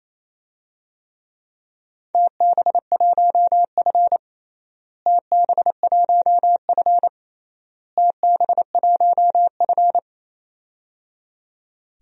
Morse code, heard as a keyed tone, about 28 wpm, 700 Hz. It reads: TB1F TB1F TB1F
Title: DXCC-100 Call Signs - 4 Characters - Repeated 3x - 28wpm